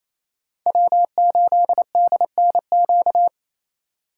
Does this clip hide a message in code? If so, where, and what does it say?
Yes: Morse code from 0.7 to 3.3 s, heard as a keyed tone, about 36 wpm, 700 Hz. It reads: W8DNQ